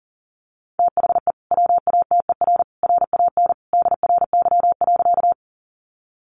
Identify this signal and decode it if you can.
Morse code, heard as a keyed tone, about 40 wpm, 700 Hz. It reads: THE WATER RAN DRY.
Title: Sentences from Top 500 Words 40wpm